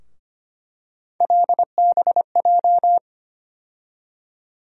Morse code, heard as a keyed tone, about 25 wpm, 700 Hz. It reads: LBJ
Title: USA Call Sign Postfixes - 3 Letters 25wpm